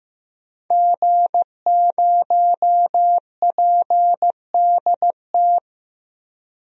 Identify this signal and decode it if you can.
Morse code, heard as a keyed tone, about 15 wpm, 700 Hz. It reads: G0PDT
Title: DXCC-100 Call Signs - 5 Characters - 15wpm